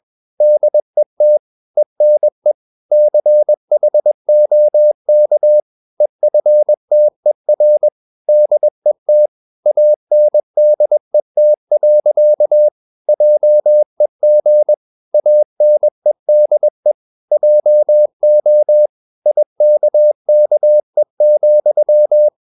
Morse code, heard as a keyed tone, about 21 wpm, 600 Hz. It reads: DET ENE CHOK EFTER DET ANDET. JEG ANEDE JO IKKE,
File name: Ferie QRS 21wpm